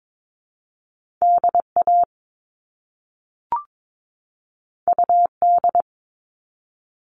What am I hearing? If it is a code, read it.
Morse code, heard as a keyed tone, about 22 wpm, 700 Hz. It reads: DA UD
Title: All Cut Numbers - 2-Digit 22wpm